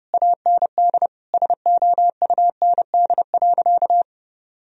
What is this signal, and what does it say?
Morse code, heard as a keyed tone, about 30 wpm, 700 Hz. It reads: AND SOUND.
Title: Sentences from Top 1000 Words - Review Part 1 - 30wpm